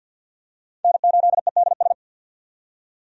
Morse code, heard as a keyed tone, about 47 wpm, 700 Hz. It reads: N8EDS
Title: USA Call Signs - 5 Characters - 50wpm